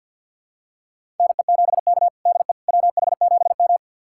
Morse code, heard as a keyed tone, about 53 wpm, 700 Hz. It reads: NE7K DE WH7M
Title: USA Callsign DE Callsign - 4 Characters - 50wpm